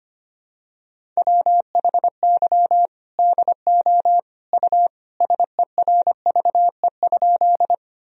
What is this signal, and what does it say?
Morse code, heard as a keyed tone, about 25 wpm, 700 Hz. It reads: WHY DO U SERVE?
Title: Sentences from Top 400 Words and Abbreviations 25wpm